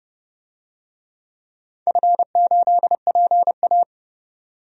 Morse code, heard as a keyed tone, about 30 wpm, 700 Hz. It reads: F8PA
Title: DXCC-100 Call Signs - 4 Characters - 30wpm